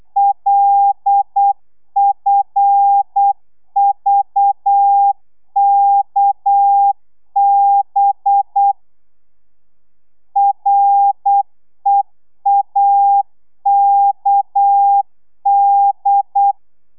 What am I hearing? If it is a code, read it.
Morse code, heard as a keyed tone, about 8 wpm, 800 Hz. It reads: LFVKB REAKD
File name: telegrafia / i2tfj / Lez6 35